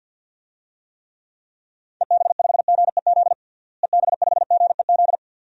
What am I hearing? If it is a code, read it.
Morse code, heard as a keyed tone, about 47 wpm, 700 Hz. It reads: EB5GEZ EB5GEZ
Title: DXCC-100 Call Signs - 6 Characters - Repeated 2x - 50wpm